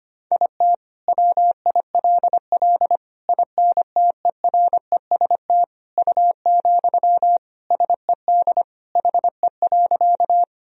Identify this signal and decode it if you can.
Morse code, heard as a keyed tone, about 25 wpm, 700 Hz. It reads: IT WILL INTEREST U, SED HE.
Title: Sentences from Top 400 Words and Abbreviations 25wpm